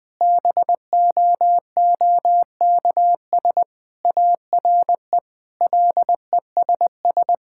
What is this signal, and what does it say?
Morse code, heard as a keyed tone, about 20 wpm, 700 Hz. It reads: BOOKS ARE LESS
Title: Sentences from Top 1000 Words - Review Part 6 - 20wpm